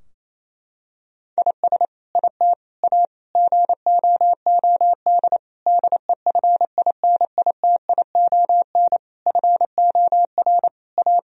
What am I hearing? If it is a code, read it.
Morse code, heard as a keyed tone, about 28 wpm, 700 Hz. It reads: IS IT A GOOD DEFINITION FOR A
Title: Sentences from Top 1000 Words - Review Part 4 - 28wpm